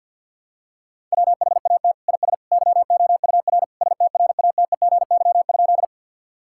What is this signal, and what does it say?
Morse code, heard as a keyed tone, about 52 wpm, 700 Hz. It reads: WHAT IS YOUR STRATEGY?
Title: Sentences from Top 1000 Words - Review Part 5 - 50wpm